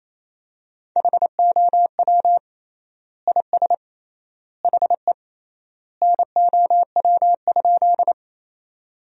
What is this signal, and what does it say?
Morse code, heard as a keyed tone, about 28 wpm, 700 Hz. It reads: HOW IS HE NOW?
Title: 3x Word-Spacing - Sentences from Top 100 Words - 28wpm